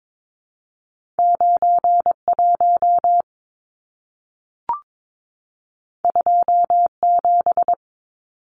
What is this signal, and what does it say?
Morse code, heard as a keyed tone, about 22 wpm, 700 Hz. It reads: 91 27
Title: Numbers - 2-Digit 22wpm